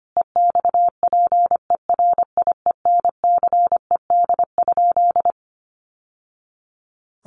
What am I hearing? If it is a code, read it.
Morse code, heard as a keyed tone, about 25 wpm, 700 Hz. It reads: EXPERIENCED?
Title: Sentences from Top 1100 Words 25wpm